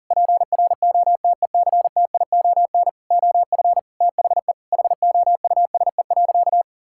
Morse code, heard as a keyed tone, about 40 wpm, 700 Hz. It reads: PROTECTION OF THE HOUSE.
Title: Sentences from Top 1600 Words 40wpm